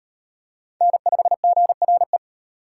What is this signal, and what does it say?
Morse code, heard as a keyed tone, about 34 wpm, 700 Hz. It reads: N5GRE